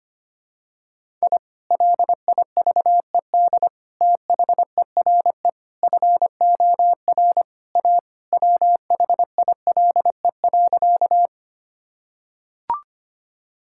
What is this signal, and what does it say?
Morse code, heard as a keyed tone, about 25 wpm, 700 Hz. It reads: I LIVED THERE FOR A WHILE.
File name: practice / 25wpm / Sentences with 3-Word Phrases - 25wpm